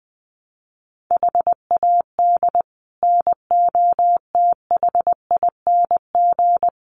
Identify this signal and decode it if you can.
Morse code, heard as a keyed tone, about 20 wpm, 700 Hz. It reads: HAD NOTHING
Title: Sets of 4 Words 20wpm